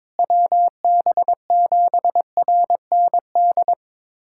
Morse code, heard as a keyed tone, about 22 wpm, 700 Hz. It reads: WB7RND
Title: USA Callsign DE Callsign - 6 Characters - 22wpm